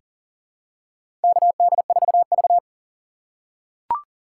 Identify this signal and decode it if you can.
Morse code, heard as a keyed tone, about 40 wpm, 700 Hz. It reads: KD4V
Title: USA Call Signs - 4 Characters - 3x Character Spacing - 40wpm